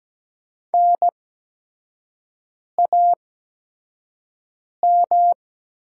Morse code, heard as a keyed tone, about 17 wpm, spaced as for 5 wpm, 700 Hz. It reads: NAM
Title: 8x Character Spacing - Top 200 Words - 17wpm